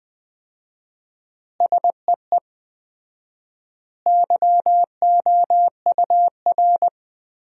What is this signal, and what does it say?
Morse code, heard as a keyed tone, about 21 wpm, 700 Hz. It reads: SEE YOUR
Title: 4x Word-Spacing - Sentences from Top 500 Words - 20wpm